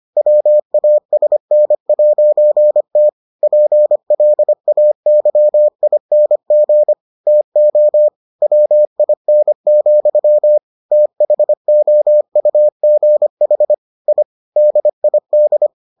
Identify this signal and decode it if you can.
Morse code, heard as a keyed tone, about 25 wpm, 600 Hz. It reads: WASN'T PLAYING TO WIN, THOUGH I DID